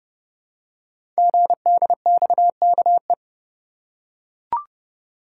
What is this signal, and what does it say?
Morse code, heard as a keyed tone, about 28 wpm, 700 Hz. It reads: GDXKE
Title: Code Groups - 5 Letters - 30wpm